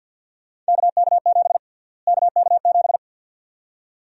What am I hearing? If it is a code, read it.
Morse code, heard as a keyed tone, about 52 wpm, 700 Hz. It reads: KK7 KK7
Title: USA Call Sign Prefixes - Repeated 2x - 50wpm